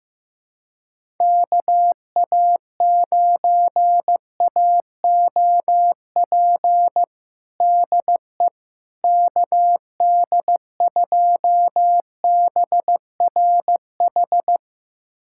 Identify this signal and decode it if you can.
Morse code, heard as a keyed tone, about 15 wpm, 700 Hz. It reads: KA9AOP DE KD2BRH
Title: USA Callsign DE Callsign - 6 Characters - 15wpm